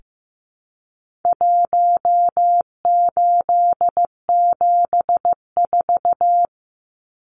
Morse code, heard as a keyed tone, about 15 wpm, 700 Hz. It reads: 1874
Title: Numbers - Contest Style - 4-Digit 15wpm